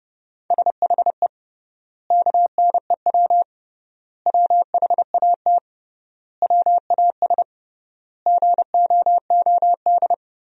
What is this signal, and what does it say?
Morse code, heard as a keyed tone, about 30 wpm, 700 Hz. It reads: SHE KNEW WHAT WAS GOOD
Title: 3x Word-Spacing - Sentences from Top 400 Words - 30wpm